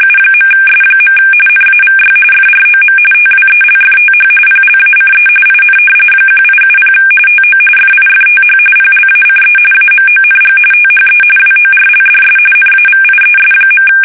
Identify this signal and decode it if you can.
RTTY, about 45.45 baud. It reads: U.S. NAVY FROM TWELFTH PLACE IN 1870 TO FIFTH PLACE AMONG THE WORLDS NAVIES.
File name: fsk850 / usnavy3